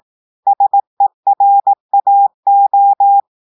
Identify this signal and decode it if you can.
Morse code, heard as a keyed tone, about 18 wpm, 800 Hz. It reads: SERAO
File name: audios / portugais / portugais texte en clair 2 vitesse 18 mots minutes